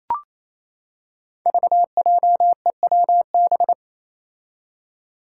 Morse code, heard as a keyed tone, about 28 wpm, 700 Hz. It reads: VJEWB
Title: Five Challenging Letter-Number Combinations 28wpm